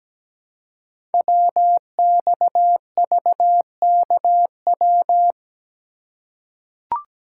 Morse code, heard as a keyed tone, about 17 wpm, 700 Hz. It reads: WXVKW
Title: Code Groups - 5 Letters - 17wpm